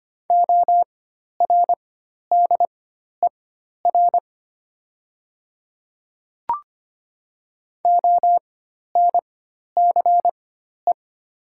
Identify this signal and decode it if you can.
Morse code, heard as a keyed tone, about 25 wpm, spaced as for 12 wpm, 700 Hz. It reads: ORDER ONCE
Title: 4x Character Spacing - Top 500 Words - 25wpm